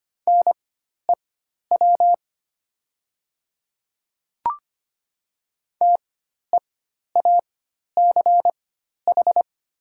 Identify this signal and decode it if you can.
Morse code, heard as a keyed tone, about 25 wpm, spaced as for 12 wpm, 690 Hz. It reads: NEW TEACH
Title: 4x Character Spacing - Top 500 Words - 25wpm